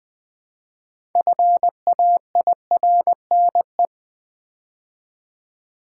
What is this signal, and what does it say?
Morse code, heard as a keyed tone, about 20 wpm, 700 Hz. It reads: FAIRNE
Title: Build A Word - 3 Letters At A Time - 8-Letter Words - 20wpm